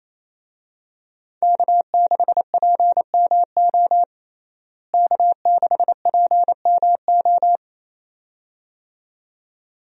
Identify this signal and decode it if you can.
Morse code, heard as a keyed tone, about 27 wpm, 700 Hz. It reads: K6PMO K6PMO
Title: USA Call Signs - 5 Characters - Repeated 2x - 28wpm